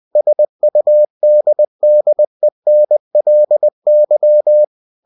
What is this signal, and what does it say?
Morse code, heard as a keyed tone, about 20 wpm, 600 Hz. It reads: SUDDENLY